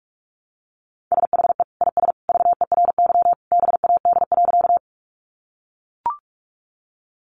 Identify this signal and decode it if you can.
Morse code, heard as a keyed tone, about 44 wpm, 700 Hz. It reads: SHE IS VERY BAD.
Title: Sentences from Top 700 Words 45wpm